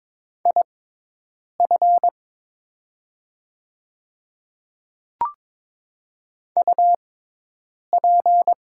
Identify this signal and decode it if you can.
Morse code, heard as a keyed tone, about 22 wpm, spaced as for 8 wpm, 700 Hz. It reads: IF UP
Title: 6x Character Spacing - Top 100 Words - 22wpm